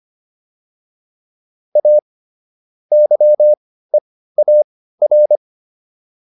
Morse code, heard as a keyed tone, about 25 wpm, spaced as for 15 wpm, 600 Hz. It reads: A YEAR